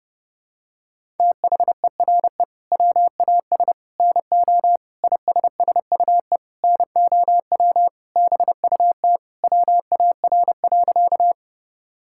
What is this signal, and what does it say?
Morse code, heard as a keyed tone, about 32 wpm, 700 Hz. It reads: THERE WAS NO ISSUE NOW BUT WAR.
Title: Sentences from Top 1000 Words - Review Part 6 - 30wpm